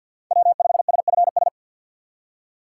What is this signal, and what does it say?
Morse code, heard as a keyed tone, about 47 wpm, 700 Hz. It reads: W5SFS
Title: USA Call Signs - 5 Characters - 6x Character Spacing - 50wpm